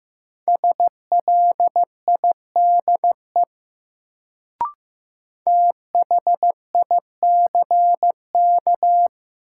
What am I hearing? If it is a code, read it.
Morse code, heard as a keyed tone, about 15 wpm, 700 Hz. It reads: SLIDE THICK
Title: Sentences from Top 1600 Words - New Word Drill Alt - 15wpm